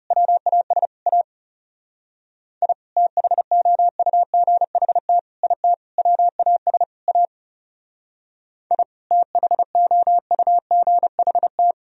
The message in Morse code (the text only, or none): WAS A I THOUGHT IT WAS A I THOUGHT